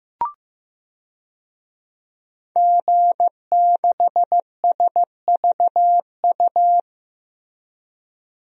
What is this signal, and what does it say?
Morse code, heard as a keyed tone, about 15 wpm, 700 Hz. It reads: G6SVU
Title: DXCC-100 Call Signs - 5 Characters - Repeated 3x - 15wpm